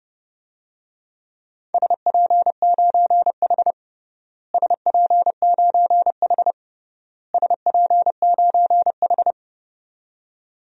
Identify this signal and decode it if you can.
Morse code, heard as a keyed tone, about 29 wpm, 700 Hz. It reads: SP9H SP9H SP9H
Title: DXCC-100 Call Signs - 4 Characters - Repeated 3x - 30wpm